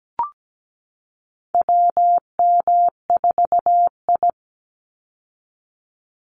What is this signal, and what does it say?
Morse code, heard as a keyed tone, about 17 wpm, 700 Hz. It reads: WM4I
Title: Call Signs Repeated 3x - 17wpm